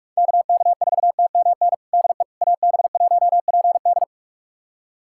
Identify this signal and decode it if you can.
Morse code, heard as a keyed tone, about 46 wpm, 700 Hz. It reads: KK4TMN DE AB1PD
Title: USA Callsign DE Callsign 45wpm